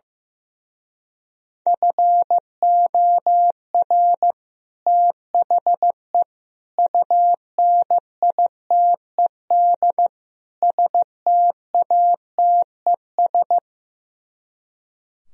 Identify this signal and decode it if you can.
Morse code, heard as a keyed tone, about 15 wpm, 700 Hz. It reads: FOR THE UNITED STATES
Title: Speed-Racing - Sets of 4 Words 15wpm